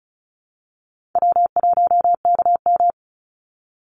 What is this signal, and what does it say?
Morse code, heard as a keyed tone, about 34 wpm, 700 Hz. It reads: W1KM
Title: Easy Call Signs Repeated 2x - 35wpm